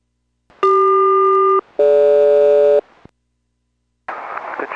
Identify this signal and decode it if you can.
Aircraft SELCAL, heard as two pairs of tones. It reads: CQ-DG